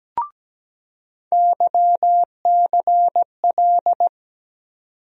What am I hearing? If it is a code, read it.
Morse code, heard as a keyed tone, about 17 wpm, 700 Hz. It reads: YCL
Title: USA Call Sign Postfixes - 3 Letters 17wpm